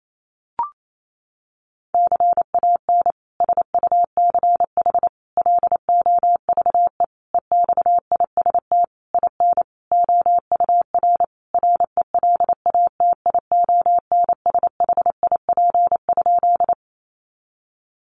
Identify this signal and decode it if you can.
Morse code, heard as a keyed tone, about 28 wpm, 700 Hz. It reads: CAN SUCH LOVE EXIST IN OUR RELATIONSHIP?
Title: Sentences from Top 1100 Words 28wpm